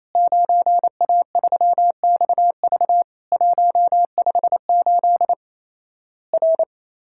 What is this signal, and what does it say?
Morse code, heard as a keyed tone, about 28 wpm, 700 Hz. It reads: 9A3XV 158 R
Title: Worked All Europe DX Contest - QTC Exchange - 28wpm